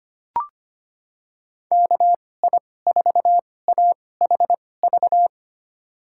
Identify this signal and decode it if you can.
Morse code, heard as a keyed tone, about 25 wpm, spaced as for 18 wpm, 700 Hz. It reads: KI4AHV
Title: USA Call Signs - 2x Character Spacing - 25wpm